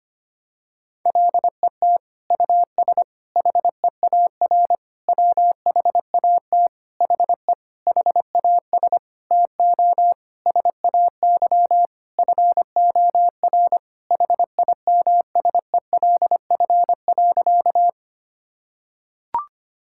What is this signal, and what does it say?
Morse code, heard as a keyed tone, about 25 wpm, 700 Hz. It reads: LET US HEAR WHAT HE HAS TO SAY FOR HIMSELF.
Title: Sentences from Top 1600 Words 25wpm